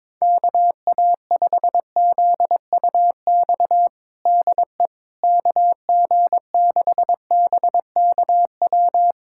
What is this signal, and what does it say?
Morse code, heard as a keyed tone, about 22 wpm, 700 Hz. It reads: KA5ZUX DE KG6BKW
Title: USA Callsign DE Callsign 22wpm